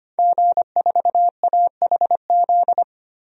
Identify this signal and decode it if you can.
Morse code, heard as a keyed tone, about 25 wpm, 700 Hz. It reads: G4AHZ
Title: DXCC-100 Call Signs - 5 Characters - 25wpm